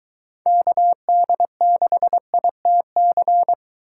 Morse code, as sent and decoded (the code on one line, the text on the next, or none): -.- -.. -.... .. - -.-.
KD6ITC